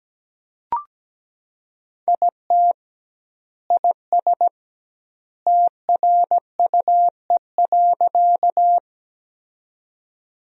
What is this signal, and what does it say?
Morse code, heard as a keyed tone, about 17 wpm, 700 Hz. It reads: IT IS TRUE.
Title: 2x Word-Spacing - Sentences from Top 400 Words - 17wpm